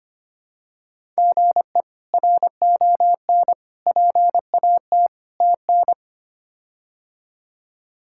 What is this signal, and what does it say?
Morse code, heard as a keyed tone, about 25 wpm, 700 Hz. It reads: GE RON PAT TN
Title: K1USN SST QSO Practice - 25wpm